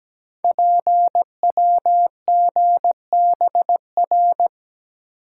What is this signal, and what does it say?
Morse code, heard as a keyed tone, about 17 wpm, 700 Hz. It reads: PWGBR
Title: Code Groups - 5 Letters - 17wpm